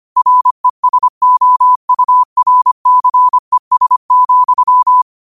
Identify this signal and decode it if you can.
Morse code, heard as a keyed tone, about 25 wpm, 1000 Hz. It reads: RESOURCES,
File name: qotdfiles / 25wpm